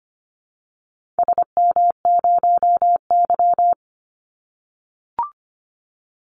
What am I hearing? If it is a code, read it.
Morse code, heard as a keyed tone, about 25 wpm, 700 Hz. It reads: SM0Y
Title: Call Signs - 25wpm